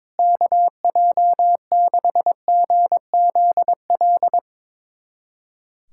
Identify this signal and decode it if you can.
Morse code, heard as a keyed tone, about 22 wpm, 700 Hz. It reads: KJ6GZL
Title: USA Callsign DE Callsign 22wpm